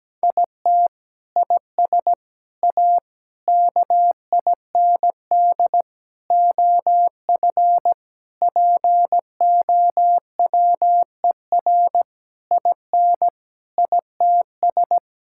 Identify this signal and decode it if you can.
Morse code, heard as a keyed tone, about 17 wpm, 700 Hz. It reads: IT IS A KIND OF POWER IN ITS
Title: Sentences from Top 1000 Words - Review Part 5 - 17wpm